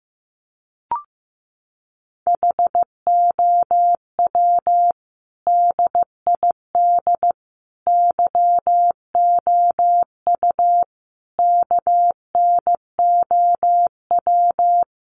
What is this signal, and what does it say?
Morse code, heard as a keyed tone, about 15 wpm, 700 Hz. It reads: HOW DID YOU KNOW
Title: Sentences from Top 100 Words 15wpm